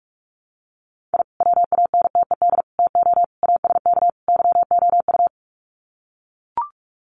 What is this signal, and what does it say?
Morse code, heard as a keyed tone, about 44 wpm, 700 Hz. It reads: I WANTED TO ASK YOU
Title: Sets of 5 Words Encore2 45wpm